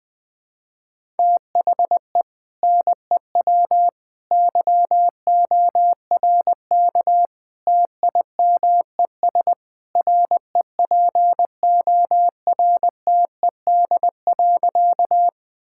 Morse code, heard as a keyed tone, about 20 wpm, 700 Hz. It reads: THE NEW YORK TIMES REPORTED.
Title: Sentences with 3-Word Phrases - 20wpm